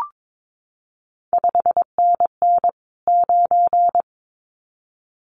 Morse code, heard as a keyed tone, about 22 wpm, 700 Hz. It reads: 5NN 9
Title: CQ WW DX Contest Exchange - 22wpm